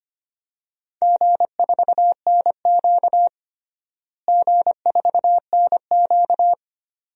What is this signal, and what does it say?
Morse code, heard as a keyed tone, about 25 wpm, 700 Hz. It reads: G4NQ G4NQ
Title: DXCC-100 Call Signs - 4 Characters - Repeated 2x - 25wpm